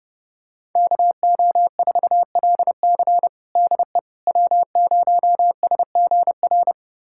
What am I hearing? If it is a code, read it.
Morse code, heard as a keyed tone, about 28 wpm, 700 Hz. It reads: KO4LC DE W0SGR
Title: USA Callsign DE Callsign - 5 Characters - 30wpm